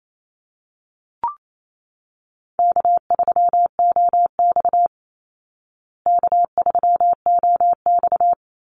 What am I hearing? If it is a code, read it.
Morse code, heard as a keyed tone, about 28 wpm, 700 Hz. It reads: K3OX K3OX
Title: Call Signs Repeated 2x - 28wpm